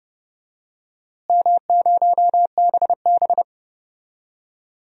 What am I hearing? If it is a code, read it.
Morse code, heard as a keyed tone, about 31 wpm, 700 Hz. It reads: M0BB
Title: DXCC-100 Call Signs - 4 Characters - Repeated 3x - 30wpm